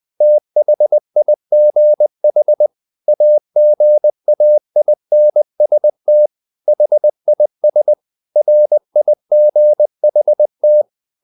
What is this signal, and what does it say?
Morse code, heard as a keyed tone, about 20 wpm, 600 Hz. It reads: THIGH AGAINST HIS RIGHT
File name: the old man and the sea part 2 20wpm